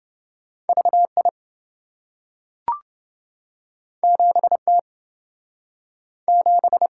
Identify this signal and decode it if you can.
Morse code, heard as a keyed tone, about 27 wpm, 700 Hz. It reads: VI 7T 7